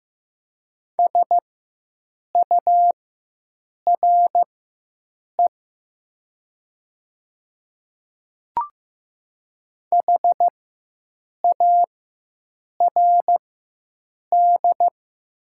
Morse code, heard as a keyed tone, about 15 wpm, spaced as for 7 wpm, 700 Hz. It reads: SURE HARD
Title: 4x Character Spacing - Top 300 Words - 15wpm